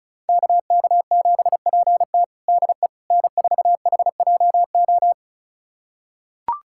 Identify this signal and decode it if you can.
Morse code, heard as a keyed tone, about 36 wpm, 700 Hz. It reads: KK7PT DE N4HJO
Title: USA Callsign DE Callsign - 5 Characters - 35wpm